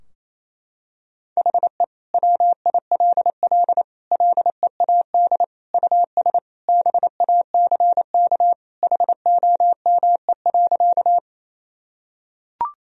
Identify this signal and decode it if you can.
Morse code, heard as a keyed tone, about 28 wpm, 700 Hz. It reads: HE WILL LEAD US BACK HOME.
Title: Sentences from Top 1000 Words - Review Part 2 - 28wpm